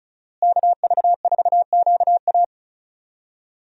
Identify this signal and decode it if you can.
Morse code, heard as a keyed tone, about 34 wpm, 700 Hz. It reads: KV4QA
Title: USA Call Signs - 5 Characters - Repeated 3x - 35wpm